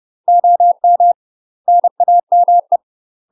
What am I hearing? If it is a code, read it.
Morse code, heard as a keyed tone, about 30 wpm, 700 Hz. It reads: OM NAME
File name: UK MASTER QSOs TRAINING / 30WPM / UK MASTER QSOs 0001